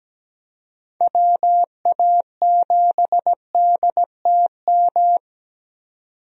Morse code, heard as a keyed tone, about 17 wpm, 700 Hz. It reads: WA7DTM
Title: USA Call Signs - 6 Characters - 17wpm